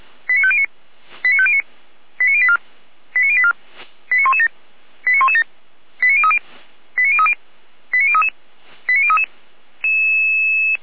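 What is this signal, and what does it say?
Five-tone selective call (ZVEI), heard as quick runs of five tones, tones of about 70 ms. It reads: 89509 89509 89084 89084 89117 89117 89030 89030 89033 89033
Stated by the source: ZVEI five-tone selective call 89092, received by SDR.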